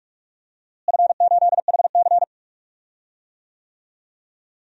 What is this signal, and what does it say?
Morse code, heard as a keyed tone, about 42 wpm, 700 Hz. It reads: F8HC
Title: DXCC-100 Call Signs - 4 Characters - Repeated 2x - 45wpm